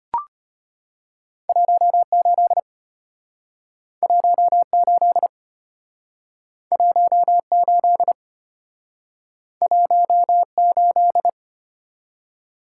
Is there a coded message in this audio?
Yes: Morse code from 1.5 to 11.3 s, heard as a keyed tone, about 34 wpm, 700 Hz. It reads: 18 18 18 18